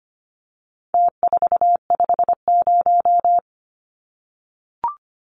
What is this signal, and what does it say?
Morse code, heard as a keyed tone, about 25 wpm, 700 Hz. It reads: T450
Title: Numbers - Contest Style - 4-Digit 25wpm